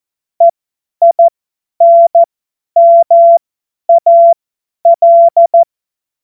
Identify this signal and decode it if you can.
Morse code, heard as a keyed tone, about 14 wpm, spaced as for 10 wpm, 680 Hz. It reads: EINMAL
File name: Morsekurs / Witze / Witz 10.Oktober 2023 50 F70